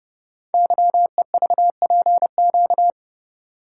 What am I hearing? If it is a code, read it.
Morse code, heard as a keyed tone, about 30 wpm, 700 Hz. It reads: YEVPQ
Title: Code Groups - 5 Letters - 30wpm